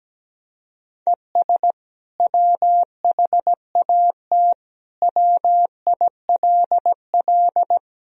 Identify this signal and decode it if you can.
Morse code, heard as a keyed tone, about 17 wpm, 700 Hz. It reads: ES WHAT WILL
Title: Sentences from Top 200 Words and Abbreviations 17wpm